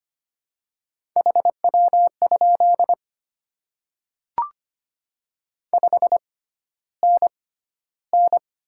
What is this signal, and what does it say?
Morse code, heard as a keyed tone, about 25 wpm, spaced as for 13 wpm, 700 Hz. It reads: HW? 5NN
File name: Top 20 QSO Elements - 6x Character Spacing - 25wpm